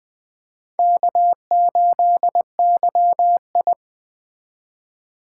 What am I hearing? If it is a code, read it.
Morse code, heard as a keyed tone, about 20 wpm, 700 Hz. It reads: K8YI
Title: USA Call Signs - 4 Characters - 4x Character Spacing - 20wpm